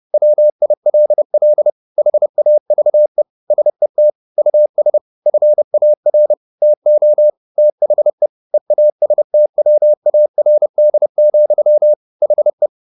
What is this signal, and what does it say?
Morse code, heard as a keyed tone, about 30 wpm, 600 Hz. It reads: WILL HAVE SET US FAR TO THE EASTWARD, HE
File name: the old man and the sea / the old man and the sea 1 30wpm